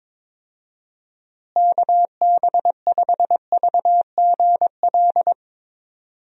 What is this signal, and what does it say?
Morse code, heard as a keyed tone, about 22 wpm, 700 Hz. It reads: KB5VGL
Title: USA Call Signs - 6 Characters - Repeated 3x - 22wpm